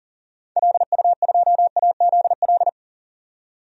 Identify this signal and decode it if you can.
Morse code, heard as a keyed tone, about 40 wpm, 700 Hz. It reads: LU2AZL
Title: DXCC-100 Call Signs - 6 Characters - 40wpm